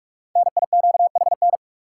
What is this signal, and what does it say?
Morse code, heard as a keyed tone, about 41 wpm, 700 Hz. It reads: NIQHN
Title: Code Groups - 5 Letter:Numbers - 45wpm